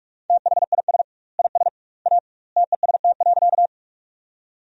Morse code, heard as a keyed tone, about 41 wpm, 700 Hz. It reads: THIS IS A TEST.
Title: 2x Word-Spacing - Sentences from Top 500 Words - 45wpm